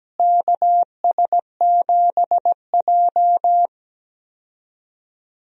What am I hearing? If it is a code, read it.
Morse code, heard as a keyed tone, about 17 wpm, 700 Hz. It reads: KS7J
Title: USA Call Signs - 4 Characters - Repeated 2x - 17wpm